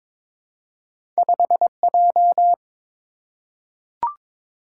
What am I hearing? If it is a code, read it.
Morse code, heard as a keyed tone, about 22 wpm, 700 Hz. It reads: 5J
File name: practice / 22wpm / Two Letter-Number Combinations - 22wpm